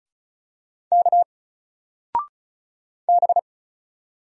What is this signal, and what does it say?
Morse code, heard as a keyed tone, about 36 wpm, 700 Hz. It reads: K B
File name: Single-Letter-Number-35wpm